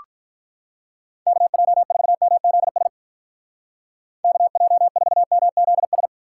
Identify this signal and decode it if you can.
Morse code, heard as a keyed tone, about 47 wpm, 700 Hz. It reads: KJ4MZS KJ4MZS